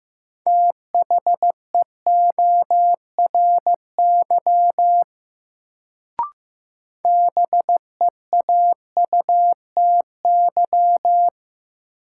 Morse code, heard as a keyed tone, about 15 wpm, 700 Hz. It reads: THEORY BEAUTY
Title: Sentences from Top 600 Words - New Word Drill - 15wpm